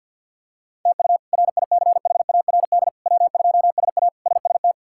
Morse code, heard as a keyed tone, about 51 wpm, 700 Hz. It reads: TU RICHARD W2SA SST
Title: K1USN SST QSO Practice - 50wpm